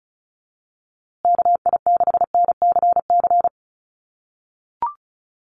Morse code, heard as a keyed tone, about 34 wpm, 700 Hz. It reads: KI6NCC
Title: USA Call Signs Repeated 3x 35wpm